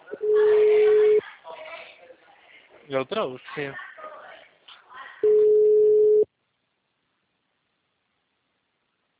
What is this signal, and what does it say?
Ringback tone (Europe, 425 Hz), with speech.